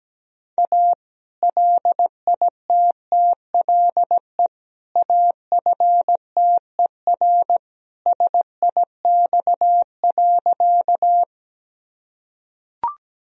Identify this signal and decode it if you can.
Morse code, heard as a keyed tone, about 17 wpm, 700 Hz. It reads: A LITTLE AFTER SIX.
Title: Sentences from Top 1000 Words - Review Part 2 - 17wpm